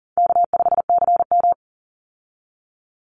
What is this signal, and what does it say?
Morse code, heard as a keyed tone, about 40 wpm, 700 Hz. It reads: K5CM
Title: Easy Call Signs - 40wpm